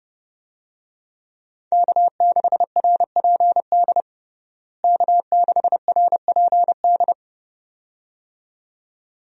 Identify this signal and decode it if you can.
Morse code, heard as a keyed tone, about 30 wpm, 700 Hz. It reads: K6RPD K6RPD
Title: USA Call Signs - 5 Characters - Repeated 2x - 30wpm